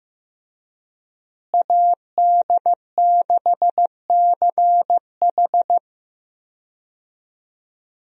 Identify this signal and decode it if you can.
Morse code, heard as a keyed tone, about 15 wpm, 700 Hz. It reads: AD6CH
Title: USA Call Signs - 5 Characters - Repeated 2x - 15wpm